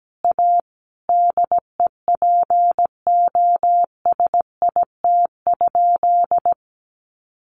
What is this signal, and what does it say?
Morse code, heard as a keyed tone, about 17 wpm, 700 Hz. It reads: A DEPOSIT?
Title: Sentences from Top 1300 Words 17wpm